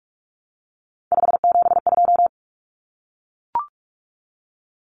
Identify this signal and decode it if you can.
Morse code, heard as a keyed tone, about 44 wpm, 700 Hz. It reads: 572
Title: Common Cut Numbers - 3-Digit 45wpm